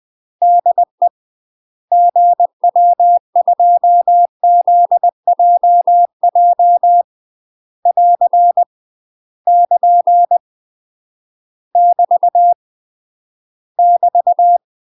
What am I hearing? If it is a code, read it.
Morse code, heard as a keyed tone, about 20 wpm, 700 Hz. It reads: DE GW2ZJJ <AR> <KN> <BT> <BT>
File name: UK MASTER QSOs TRAINING / 20WPM / UK MASTER QSOs 0032